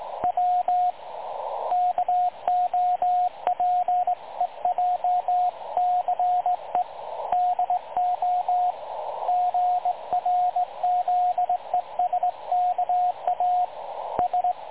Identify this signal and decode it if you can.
Morse code, heard as a keyed tone, about 19 wpm, 720 Hz, with radio noise. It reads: W KOPEJCE DO GRZESKA S
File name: dlazaawansowanych / nagraniazpasma / websdr recording 3565.0kHz